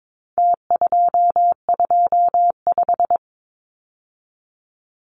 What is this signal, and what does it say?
Morse code, heard as a keyed tone, about 22 wpm, 700 Hz. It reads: T225
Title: Numbers - Contest Style - 4-Digit 22wpm